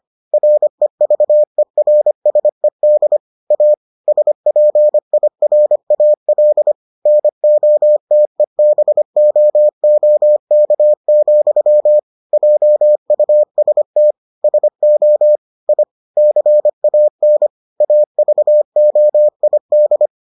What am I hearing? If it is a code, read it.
Morse code, heard as a keyed tone, about 25 wpm, 600 Hz. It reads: REVERSED A SPIRAL NOTEBOOK, JUST SO I CAN AVOID